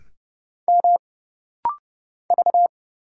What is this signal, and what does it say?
Morse code, heard as a keyed tone, about 31 wpm, 700 Hz. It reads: M V
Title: Single Letter-Number - Rapid-Fire 30wpm